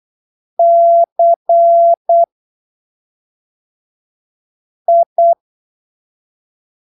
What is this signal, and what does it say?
Morse code, heard as a keyed tone, about 8 wpm, 680 Hz. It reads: C I
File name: arquivos / texto / texto 10 08ppm